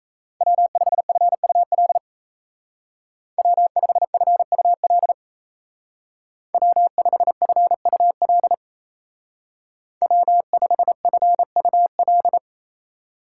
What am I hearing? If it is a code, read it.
Morse code, heard as a keyed tone, about 33 wpm, 700 Hz. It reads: W5FUL W5FUL W5FUL W5FUL
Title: Speed Racing - USA 1x3 Call Signs 28wpm